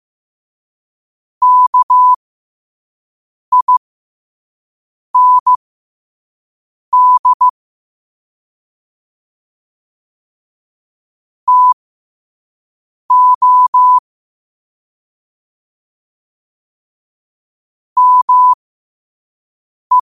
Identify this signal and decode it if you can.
Morse code, heard as a keyed tone, about 15 wpm, spaced as for 5 wpm, 1000 Hz. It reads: KIND TO ME
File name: qotdfiles / 5wpm